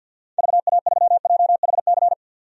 Morse code, heard as a keyed tone, about 52 wpm, 700 Hz. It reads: VA3JHC
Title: DXCC-100 Call Signs - 6 Characters - 50wpm